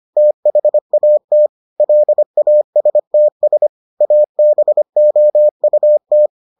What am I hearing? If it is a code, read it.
Morse code, heard as a keyed tone, about 25 wpm, 600 Hz. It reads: THAT LASTS ABOUT